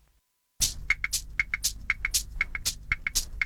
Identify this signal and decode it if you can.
Emergency Alert System (SAME) tones, heard as harsh buzzing bursts. It is heard from 0.8 to 3.2 s.